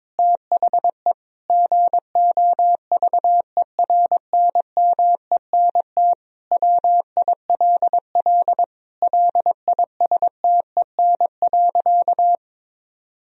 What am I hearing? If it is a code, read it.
Morse code, heard as a keyed tone, about 22 wpm, 700 Hz. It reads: THE GOVERNMENT WILL LISTEN.
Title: Sentences from Top 1000 Words - Review Part 3 - 22wpm